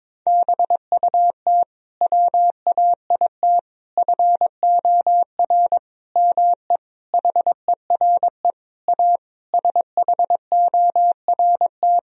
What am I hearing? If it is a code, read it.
Morse code, heard as a keyed tone, about 22 wpm, 700 Hz. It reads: BUT WAIT FOR ME HERE A SHORT
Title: Sentences from Top 1000 Words - Review Part 2 - 22wpm